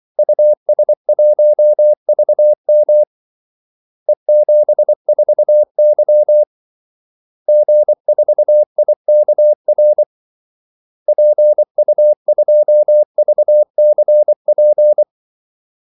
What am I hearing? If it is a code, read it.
Morse code, heard as a keyed tone, about 24 wpm, 600 Hz. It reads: US1VM E74Y G4IKR PU2VCP
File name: Callsigns-24wpm